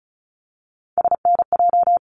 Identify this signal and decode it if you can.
Morse code, heard as a keyed tone, about 36 wpm, 700 Hz. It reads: SNJ
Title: ARRL Sections 35wpm